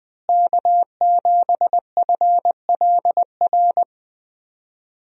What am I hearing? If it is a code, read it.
Morse code, heard as a keyed tone, about 20 wpm, 700 Hz. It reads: K7FLR
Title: USA Call Signs - 5 Characters - Repeated 3x - 20wpm